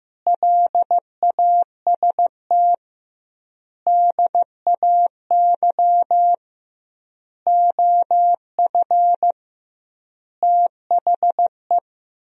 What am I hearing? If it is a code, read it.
Morse code, heard as a keyed tone, about 16 wpm, 700 Hz. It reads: LAST DAY OF THE
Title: 2x Word-Spacing - Sentences from Top 300 Words - 15wpm